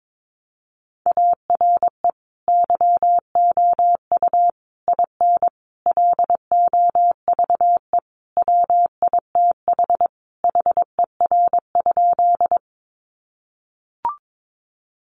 Morse code, heard as a keyed tone, about 22 wpm, 700 Hz. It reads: ARE YOU IN LOVE WITH HER?
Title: Sentences from Top 400 Words 22wpm